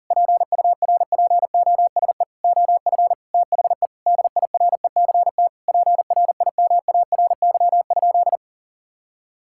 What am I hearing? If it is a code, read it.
Morse code, heard as a keyed tone, about 40 wpm, 700 Hz. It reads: PURPOSE OF THE DIRECT PRIMARY?
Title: Sentences from Top 1000 Words - Review Part 5 - 40wpm